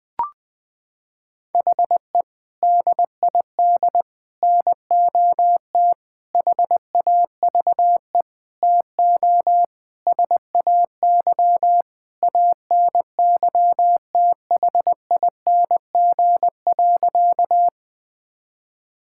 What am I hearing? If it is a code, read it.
Morse code, heard as a keyed tone, about 21 wpm, 700 Hz. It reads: HE DID NOT HAVE TO SAY ANYTHING.
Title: Sentences with 5-Word Phrases - 20wpm